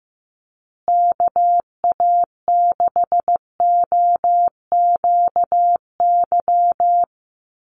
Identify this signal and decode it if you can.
Morse code, heard as a keyed tone, about 15 wpm, 700 Hz. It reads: KA6OQY
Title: USA Call Signs Repeated 2x 15wpm